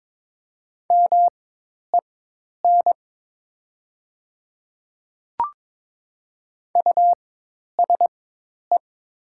Morse code, heard as a keyed tone, about 22 wpm, spaced as for 10 wpm, 700 Hz. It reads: MEN USE
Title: 4x Character Spacing - Top 200 Words - 22wpm